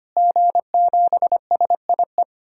Morse code, heard as a keyed tone, about 25 wpm, 700 Hz. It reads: G7SIE
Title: DXCC-100 Call Signs - 5 Characters - Repeated 2x - 25wpm